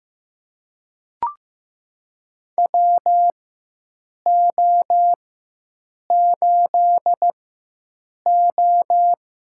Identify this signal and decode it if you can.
Morse code, heard as a keyed tone, about 15 wpm, spaced as for 7 wpm, 700 Hz. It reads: WO8O
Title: USA Call Signs - 4 Characters - 4x Character Spacing - 15wpm